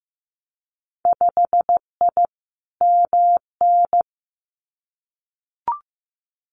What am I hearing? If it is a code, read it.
Morse code, heard as a keyed tone, about 15 wpm, 700 Hz. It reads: 5I MN
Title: Winter Field Day Exchanges Encore 15wpm